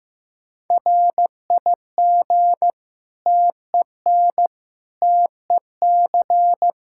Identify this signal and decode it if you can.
Morse code, heard as a keyed tone, about 15 wpm, 700 Hz. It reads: RIG TEN TEC
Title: Practice QSO - Exchange 2 - Info - 15wpm